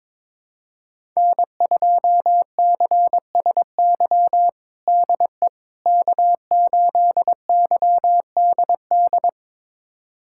Morse code, heard as a keyed tone, about 22 wpm, 700 Hz. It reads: N2CSY DE K8YDD
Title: USA Callsign DE Callsign - 5 Characters - 22wpm